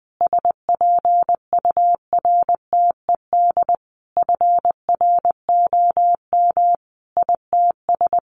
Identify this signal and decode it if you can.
Morse code, heard as a keyed tone, about 20 wpm, 700 Hz. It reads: SPURTED FROM ITS
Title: A Sound Of Thunder 20wpm